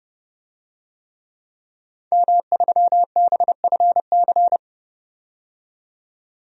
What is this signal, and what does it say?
Morse code, heard as a keyed tone, about 32 wpm, 700 Hz. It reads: M3BFC
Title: DXCC-100 Call Signs - 5 Characters - 30wpm